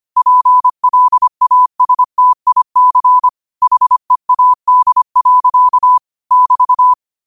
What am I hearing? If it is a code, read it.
Morse code, heard as a keyed tone, about 25 wpm, 1000 Hz. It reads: PLASTIC HEAD. <BT>
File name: qotdfiles / 25wpm